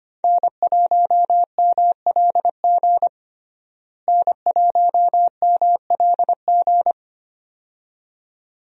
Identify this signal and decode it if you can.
Morse code, heard as a keyed tone, about 25 wpm, 700 Hz. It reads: N1MLG N1MLG
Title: USA Call Signs - 5 Characters - Repeated 2x - 25wpm